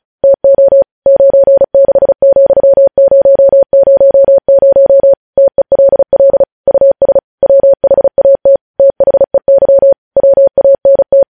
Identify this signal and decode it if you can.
Morse code, heard as a keyed tone, about 36 wpm, 550 Hz. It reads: TO 96,000 TELL US WHAT THEY WANT
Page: news headlines in morse code at 35 wpm